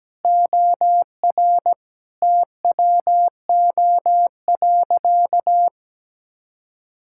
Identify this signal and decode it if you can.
Morse code, heard as a keyed tone, about 17 wpm, 700 Hz. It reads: OR TWO.
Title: Sentences from Top 1600 Words 17wpm